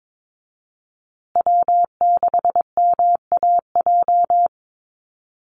Morse code, heard as a keyed tone, about 22 wpm, 700 Hz. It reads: W6MAJ
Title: USA Call Signs Repeated 3x 22wpm